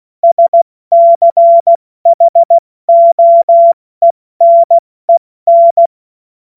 Morse code, heard as a keyed tone, about 16 wpm, 680 Hz. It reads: SCHOENEN
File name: Morsekurs / Witze / Witz 23. Dezember 2025 70 F80